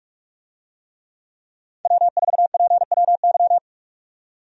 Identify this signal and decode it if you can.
Morse code, heard as a keyed tone, about 46 wpm, 700 Hz. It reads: W4PWY
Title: USA Call Signs - 5 Characters - 45wpm